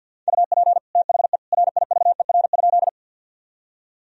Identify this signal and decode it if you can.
Morse code, heard as a keyed tone, about 48 wpm, 700 Hz. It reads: UP THE RIVER?
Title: Sentences from Top 1000 Words - Review Part 1 - 50wpm